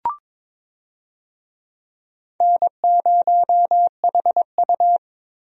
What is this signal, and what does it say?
Morse code, heard as a keyed tone, about 22 wpm, 700 Hz. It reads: N0HU
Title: USA Call Signs - 4 Characters - Repeated 3x - 22wpm